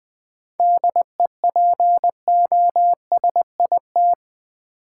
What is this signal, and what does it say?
Morse code, heard as a keyed tone, about 20 wpm, 700 Hz. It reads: DEPOSIT
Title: Sentences from Top 1300 Words - New Word Drill - 20wpm